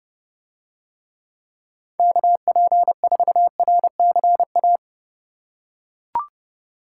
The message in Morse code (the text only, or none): KP4RCA